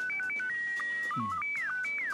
SSTV (slow-scan television), Scottie 1.